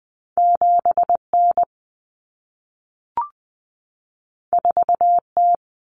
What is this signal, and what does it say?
Morse code, heard as a keyed tone, about 20 wpm, 700 Hz. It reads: 7N 4T
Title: Common Cut Numbers - 2-Digit 20wpm